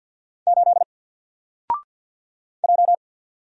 Morse code, heard as a keyed tone, about 50 wpm, 700 Hz. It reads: 8 J
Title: Single Letter-Number 50wpm